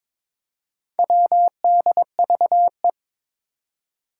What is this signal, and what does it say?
Morse code, heard as a keyed tone, about 22 wpm, 700 Hz. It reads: WDVE
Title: Four Challenging Letter-Number Combinations 22wpm